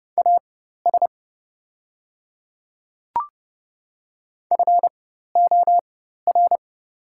Morse code, heard as a keyed tone, about 30 wpm, spaced as for 14 wpm, 700 Hz. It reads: AS FOR
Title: 4x Character Spacing - Top 100 Words - 30wpm